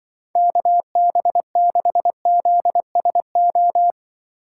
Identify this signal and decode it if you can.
Morse code, heard as a keyed tone, about 24 wpm, 700 Hz. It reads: KB6ZSO
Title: Speed Racing - USA 2x3 Call Signs 20wpm